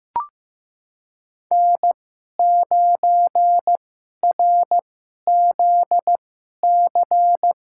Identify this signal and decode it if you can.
Morse code, heard as a keyed tone, about 15 wpm, spaced as for 11 wpm, 700 Hz. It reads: N9RZC
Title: USA Call Signs - 5 Characters - 2x Character Spacing - 15wpm